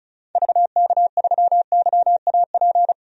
Morse code, heard as a keyed tone, about 34 wpm, 700 Hz. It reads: VK3YAP
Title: DXCC-100 Call Signs - 6 Characters - 35wpm